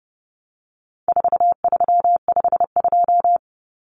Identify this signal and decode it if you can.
Morse code, heard as a keyed tone, about 30 wpm, 700 Hz. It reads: 4352
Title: Common Cut Numbers - 4-Digit 30wpm